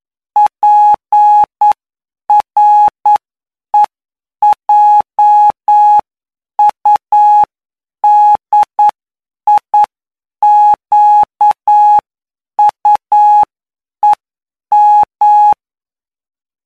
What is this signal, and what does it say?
Morse code, heard as a keyed tone, about 9 wpm, spaced as for 8 wpm, 840 Hz. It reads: PREJUDIQUEM
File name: cw / py2lcd / 20-PY2LCD-Li��o 29